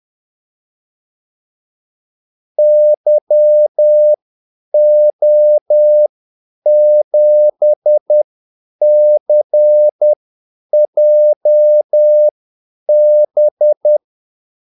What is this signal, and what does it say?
Morse code, heard as a keyed tone, about 10 wpm, spaced as for 8 wpm, 600 Hz. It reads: YO7CJB
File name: Callsigns-10-8wpm